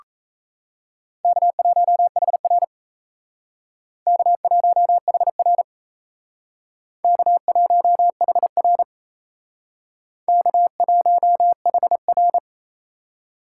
Morse code, heard as a keyed tone, about 35 wpm, 700 Hz. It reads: K1HR K1HR K1HR K1HR
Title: Speed Racing - USA 1x2 Call Signs 28wpm